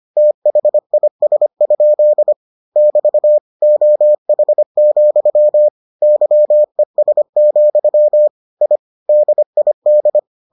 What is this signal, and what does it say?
Morse code, heard as a keyed tone, about 25 wpm, 600 Hz. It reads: THIS? <BT> OH, YES, I DID